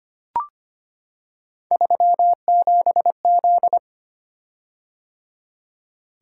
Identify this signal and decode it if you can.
Morse code, heard as a keyed tone, about 25 wpm, 700 Hz. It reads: 37Z
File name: Three Letter-Number Combinations - 25wpm